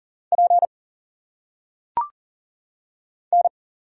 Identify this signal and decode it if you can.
Morse code, heard as a keyed tone, about 40 wpm, 700 Hz. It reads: P N
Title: Single Letters - Phonetic Pronunciation 40wpm